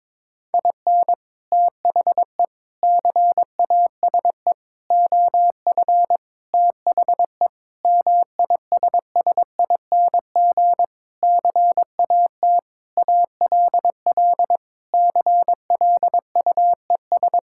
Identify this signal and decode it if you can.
Morse code, heard as a keyed tone, about 22 wpm, 700 Hz. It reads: IN THE CASE OF THE MISSING CAT ALL CLUES
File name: practice / 22wpm / Sentences with 4-Word Phrases - 22wpm